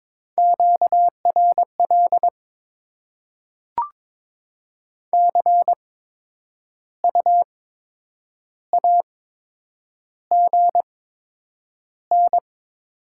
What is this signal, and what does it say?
Morse code, heard as a keyed tone, about 22 wpm, spaced as for 8 wpm, 700 Hz. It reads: QRL CU AGN
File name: Top 20 QSO Elements - 8x Character Spacing - 22wpm